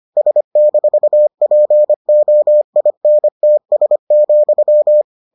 Morse code, heard as a keyed tone, about 25 wpm, 600 Hz. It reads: S-POINTS,